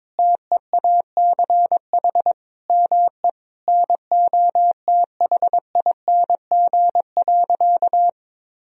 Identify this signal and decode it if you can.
Morse code, heard as a keyed tone, about 22 wpm, 700 Hz. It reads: TEACH ME NOTHING.
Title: Sentences from Top 1000 Words - Review Part 2 - 22wpm